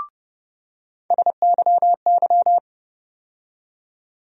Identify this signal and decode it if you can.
Morse code, heard as a keyed tone, about 31 wpm, 700 Hz. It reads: SYY
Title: USA Call Sign Postfixes - 3 Letters 30wpm